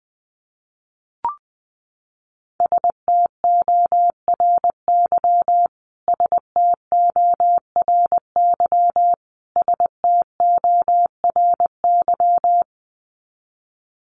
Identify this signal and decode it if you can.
Morse code, heard as a keyed tone, about 20 wpm, 700 Hz. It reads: STORY STORY STORY
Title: Top 300 Words Repeated 20wpm